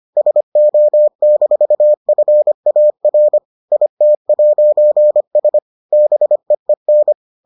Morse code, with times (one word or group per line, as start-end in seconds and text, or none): 0.2-3.4 SO-FAR
3.7-5.6 IT'S
5.9-7.1 BEEN